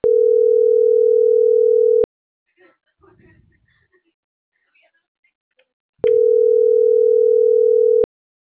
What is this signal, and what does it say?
Ringback tone (North America), 440 + 480 Hz.